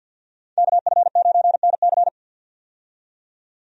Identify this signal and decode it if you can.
Morse code, heard as a keyed tone, about 52 wpm, 700 Hz. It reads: KF9NC
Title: USA Call Signs - 5 Characters - 50wpm